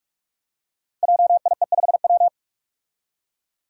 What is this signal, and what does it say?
Morse code, heard as a keyed tone, about 44 wpm, 700 Hz. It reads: JIE5W
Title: Code Groups - 5 Letter:Numbers - 45wpm